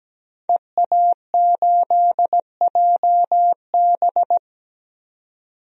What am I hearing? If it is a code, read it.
Morse code, heard as a keyed tone, about 17 wpm, 700 Hz. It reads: EA8JB
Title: DXCC-100 Call Signs - 5 Characters - 17wpm